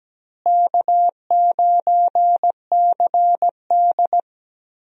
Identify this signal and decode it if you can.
Morse code, heard as a keyed tone, about 17 wpm, 700 Hz. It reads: K9CD
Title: USA Call Signs - 4 Characters - Repeated 2x - 17wpm